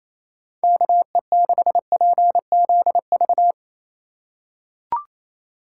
Morse code, heard as a keyed tone, about 28 wpm, 700 Hz. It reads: KE6PZV
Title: USA Call Signs - 3x Character Spacing - 28wpm